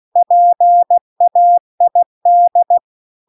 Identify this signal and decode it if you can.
Morse code, heard as a keyed tone, about 16 wpm, 700 Hz. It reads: PAID
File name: CW-The-Big-Twoheaded-River-2-16WPM